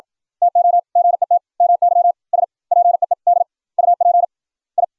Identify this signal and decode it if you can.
Morse code, heard as a keyed tone, about 56 wpm, spaced as for 47 wpm, 700 Hz. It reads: TO GET MY SPEED UP I